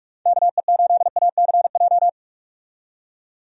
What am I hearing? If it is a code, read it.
Morse code, heard as a keyed tone, about 46 wpm, 700 Hz. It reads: KE8ACJ
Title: USA Call Signs - 6 Characters - 45wpm